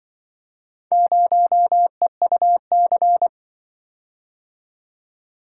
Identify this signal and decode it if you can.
Morse code, heard as a keyed tone, about 23 wpm, 700 Hz. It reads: 0EUC
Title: Speed-Racing - Four Letter-Number Combinations - 20wpm